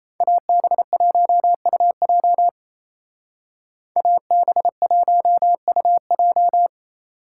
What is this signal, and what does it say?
Morse code, heard as a keyed tone, about 33 wpm, 700 Hz. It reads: AB1UJ AB1UJ
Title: Speed Racing - USA Call Signs 28wpm